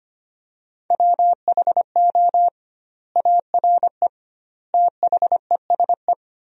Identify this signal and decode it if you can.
Morse code, heard as a keyed tone, about 25 wpm, 700 Hz. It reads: WHO ARE THESE
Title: 2x Word-Spacing - Sentences from Top 200 Words - 25wpm